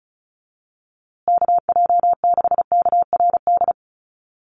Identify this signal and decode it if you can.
Morse code, heard as a keyed tone, about 34 wpm, 700 Hz. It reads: KJ6KRD
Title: USA Call Signs Repeated 2x 35wpm